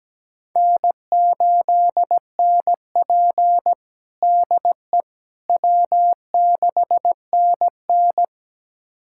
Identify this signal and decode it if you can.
Morse code, heard as a keyed tone, about 17 wpm, 700 Hz. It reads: N8NP DE W6NN
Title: USA Callsign DE Callsign - 4 Characters - 17wpm